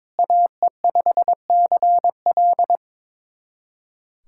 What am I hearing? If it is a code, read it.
Morse code, heard as a keyed tone, about 22 wpm, 700 Hz. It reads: AE5CL
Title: USA Callsign DE Callsign - 5 Characters - 22wpm